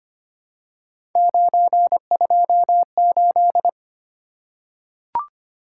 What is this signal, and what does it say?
Morse code, heard as a keyed tone, about 25 wpm, 700 Hz. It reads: 928
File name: Three Letter-Number Combinations - 25wpm